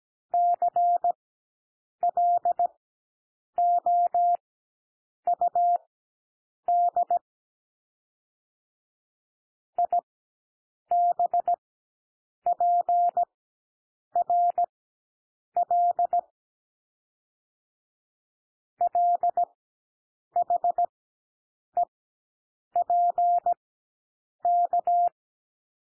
Morse code, heard as a keyed tone, about 17 wpm, spaced as for 7 wpm, 700 Hz. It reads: CLOUD IBPRL LHEPK
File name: SOUND LETTERS 17 wpm